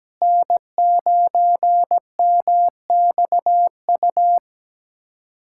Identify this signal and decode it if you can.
Morse code, heard as a keyed tone, about 17 wpm, 700 Hz. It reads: N9MXU
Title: USA Callsign DE Callsign - 5 Characters - 17wpm